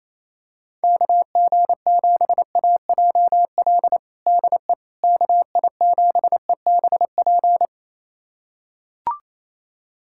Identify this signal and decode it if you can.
Morse code, heard as a keyed tone, about 28 wpm, 700 Hz. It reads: KG7AJL DE KI7EBP
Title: USA Callsign DE Callsign - 6 Characters - 28wpm